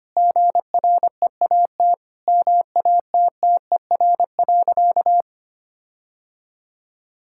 Morse code, heard as a keyed tone, about 25 wpm, 700 Hz. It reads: GREAT MATTER.
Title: Sentences from Top 1600 Words 25wpm